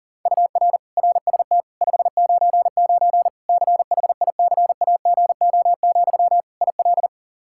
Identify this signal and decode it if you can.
Morse code, heard as a keyed tone, about 40 wpm, 700 Hz. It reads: UR RST 599 CHICAGO, IL
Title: Practice QSO - Exchange 1 - Beginning - 40wpm